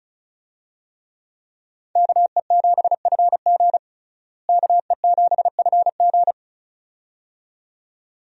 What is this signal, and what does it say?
Morse code, heard as a keyed tone, about 36 wpm, 700 Hz. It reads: KE7FG KE7FG
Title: USA Call Signs - 5 Characters - Repeated 2x - 35wpm